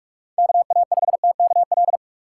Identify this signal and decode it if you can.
Morse code, heard as a keyed tone, about 46 wpm, 700 Hz. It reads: K A 5 T K L
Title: USA Call Signs - 6 Characters - 6x Character Spacing - 45wpm